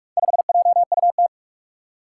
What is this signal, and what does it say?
Morse code, heard as a keyed tone, about 44 wpm, 700 Hz. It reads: 5JUT
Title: Four Challenging Letter-Number Combinations 45wpm